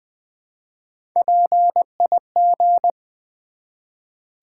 Morse code, heard as a keyed tone, about 20 wpm, 700 Hz. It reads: PIG
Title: Build A Word - 1 Letter At A Time - 3-Letter Words - 20wpm